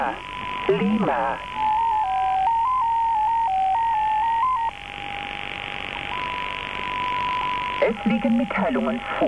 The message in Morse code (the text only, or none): none